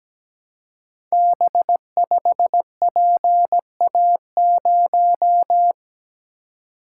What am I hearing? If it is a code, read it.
Morse code, heard as a keyed tone, about 17 wpm, 700 Hz. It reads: B5PA0
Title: Code Groups - 5 Letter:Numbers - 17wpm